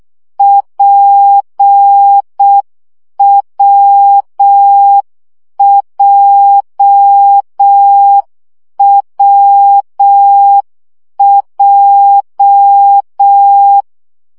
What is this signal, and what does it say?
Morse code, heard as a keyed tone, about 6 wpm, 800 Hz. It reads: PWJWJ
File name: CW gyakorlas w p j 5 baud